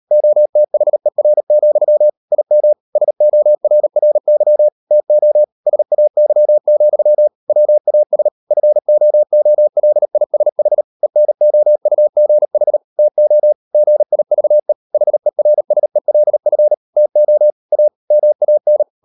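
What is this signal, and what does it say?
Morse code, heard as a keyed tone, about 38 wpm, 600 Hz. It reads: OTHER, IM SORRY TO SAY, WAS FOOLISH ENOUGH TO GIVE HERSELF TO A MAN
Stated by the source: Morse code practice at 38 WPM, sending Ken Follett's novel Never in English, from CW med Gnister.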